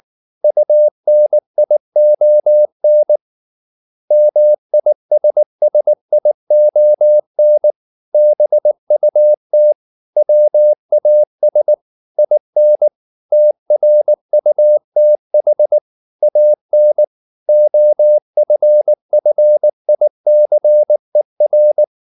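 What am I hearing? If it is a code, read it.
Morse code, heard as a keyed tone, about 19 wpm, 600 Hz. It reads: UNION MISSION BUT WAS IN TRUTH AN OFFICER
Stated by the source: Morse code at 19 wpm from CW med Gnister, reading Ken Follett's novel Never in English.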